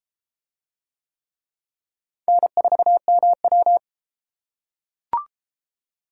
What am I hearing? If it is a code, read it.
Morse code, heard as a keyed tone, about 31 wpm, 700 Hz. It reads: N4MW